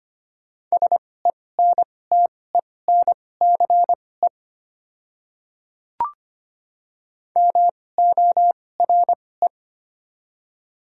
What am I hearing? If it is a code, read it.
Morse code, heard as a keyed tone, about 25 wpm, spaced as for 18 wpm, 690 Hz. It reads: SENTENCE MORE
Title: 2x Character Spacing - Top 200 Words - 25wpm